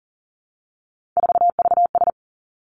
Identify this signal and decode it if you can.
Morse code, heard as a keyed tone, about 40 wpm, 700 Hz. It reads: 4VS
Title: Three Challenging Letter-Number Combinations 40wpm